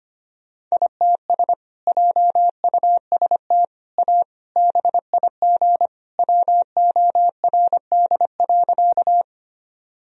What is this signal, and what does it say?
Morse code, heard as a keyed tone, about 25 wpm, 700 Hz. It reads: ITS JUST A BIG WORD.
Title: Sentences from Top 1000 Words - Review Part 1 - 25wpm